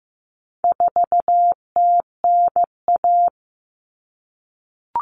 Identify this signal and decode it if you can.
Morse code, heard as a keyed tone, about 15 wpm, 700 Hz. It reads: 4TNA
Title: Common Cut Numbers - 4-Digit 15wpm